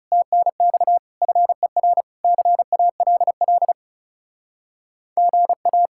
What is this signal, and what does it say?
Morse code, heard as a keyed tone, about 40 wpm, 700 Hz. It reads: TNX FER CALL GA